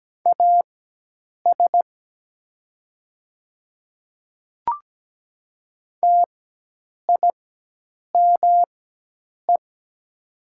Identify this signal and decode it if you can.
Morse code, heard as a keyed tone, about 17 wpm, spaced as for 8 wpm, 700 Hz. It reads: AS TIME
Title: 4x Character Spacing - Top 100 Words - 17wpm